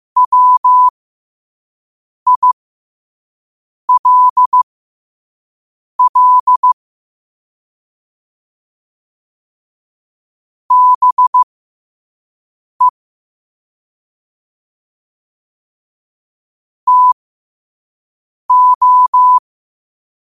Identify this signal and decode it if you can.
Morse code, heard as a keyed tone, about 15 wpm, spaced as for 5 wpm, 1000 Hz. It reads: WILL BE TO
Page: New quotes every day in morse code at 5 Words per minute.